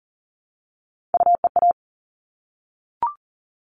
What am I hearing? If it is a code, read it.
Morse code, heard as a keyed tone, about 40 wpm, 700 Hz. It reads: UEA
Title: All Cut Numbers - 3-Digit 40wpm